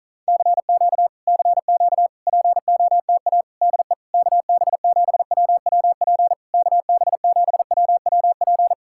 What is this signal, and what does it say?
Morse code, heard as a keyed tone, about 46 wpm, 700 Hz. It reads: CQ CQ POTA DE KB7WWP KB7WWP